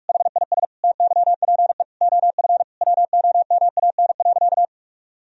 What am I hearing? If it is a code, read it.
Morse code, heard as a keyed tone, about 46 wpm, 700 Hz. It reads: HIS TYPE OF WOMAN.
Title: Sentences from Top 1000 Words - Review Part 3 - 45wpm